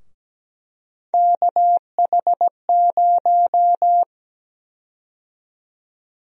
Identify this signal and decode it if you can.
Morse code, heard as a keyed tone, about 17 wpm, 700 Hz. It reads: KH0
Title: USA Call Sign Prefixes - Repeated 2x - 17wpm